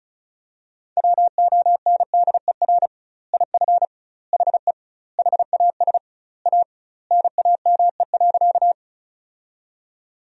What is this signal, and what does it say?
Morse code, heard as a keyed tone, about 36 wpm, 700 Hz. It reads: WONDER IF HE HAS A NAME.
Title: 2x Word-Spacing - Sentences from Top 500 Words - 35wpm